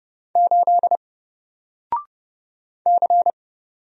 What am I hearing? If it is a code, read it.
Morse code, heard as a keyed tone, about 30 wpm, 700 Hz. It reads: 8 C
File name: Single-Letter-Number-30wpm